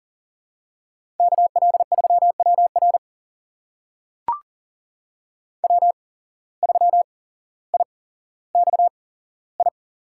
Morse code, heard as a keyed tone, about 40 wpm, spaced as for 17 wpm, 700 Hz. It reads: KL3WR W3IXI
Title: USA Call Signs - 5 Characters - 8x Character Spacing - 40wpm